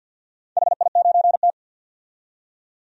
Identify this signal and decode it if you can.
Morse code, heard as a keyed tone, about 50 wpm, 700 Hz. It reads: HI9T
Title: DXCC-100 Call Signs - 4 Characters - 50wpm